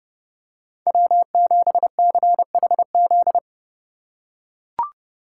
Morse code, heard as a keyed tone, about 30 wpm, 700 Hz. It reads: W7CHZ
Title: USA Call Signs - 4x Character Spacing - 30wpm